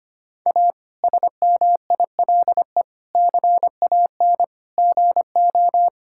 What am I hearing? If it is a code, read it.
Morse code, heard as a keyed tone, about 25 wpm, 700 Hz. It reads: A SMILE CAN GO
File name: practice / 25wpm / Difficult Sentences with Hints - 25wpm